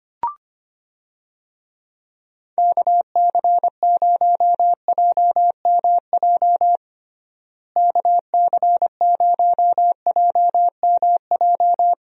Morse code, heard as a keyed tone, about 25 wpm, 700 Hz. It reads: KC0JMJ KC0JMJ
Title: USA Call Signs - 6 Characters - Repeated 3x - 25wpm